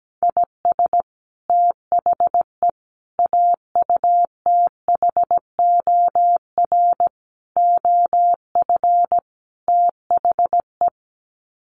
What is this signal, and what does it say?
Morse code, heard as a keyed tone, about 17 wpm, 700 Hz. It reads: IS THE AUTHOR OF THE
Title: Sets of 5 Words Encore 17wpm